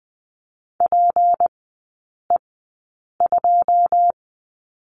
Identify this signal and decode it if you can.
Morse code, heard as a keyed tone, about 20 wpm, spaced as for 8 wpm, 700 Hz. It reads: PE2
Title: 166 Lesson 27e - Number 7 - Three Characters 20wpm